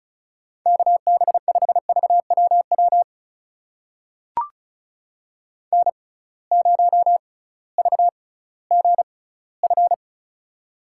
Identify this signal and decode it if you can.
Morse code, heard as a keyed tone, about 34 wpm, spaced as for 19 wpm, 700 Hz. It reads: KB5VWW N0VGF
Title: USA Call Signs - 6x Character Spacing - 35wpm